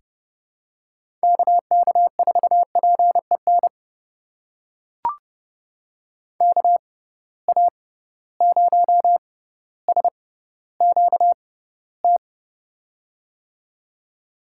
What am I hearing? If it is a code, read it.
Morse code, heard as a keyed tone, about 32 wpm, spaced as for 16 wpm, 700 Hz. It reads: KK4PEN KA0SQT
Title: USA Call Signs - 6 Characters - 6x Character Spacing - 30wpm